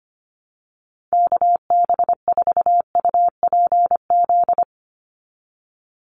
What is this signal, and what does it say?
Morse code, heard as a keyed tone, about 25 wpm, 690 Hz. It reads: KB4UPZ
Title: USA Call Signs 25wpm